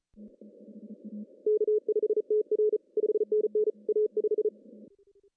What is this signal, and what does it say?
Morse code, heard as a keyed tone, about 37 wpm, 410 Hz. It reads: K5TR 5NN A5
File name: CW / cqww-cw-k5tr